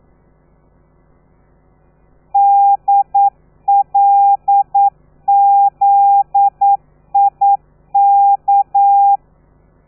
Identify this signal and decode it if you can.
Morse code, heard as a keyed tone, about 9 wpm, 800 Hz. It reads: DLZIK
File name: telegrafia / i2tfj / Lez13 40